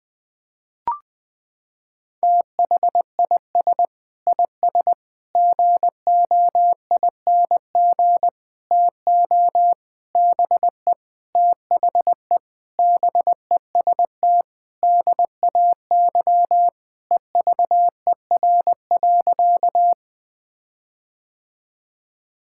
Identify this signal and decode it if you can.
Morse code, heard as a keyed tone, about 20 wpm, 700 Hz. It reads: THIS IS GOING TO BE THE BEST DAY EVER.
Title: Sentences with 5-Word Phrases - 20wpm